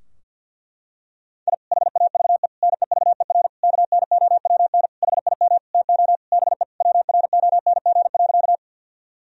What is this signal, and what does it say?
Morse code, heard as a keyed tone, about 50 wpm, 700 Hz. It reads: I HAVE NEVER KNOWN HIM TO BE WRONG.
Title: Sentences from Top 1000 Words - Review Part 3 - 50wpm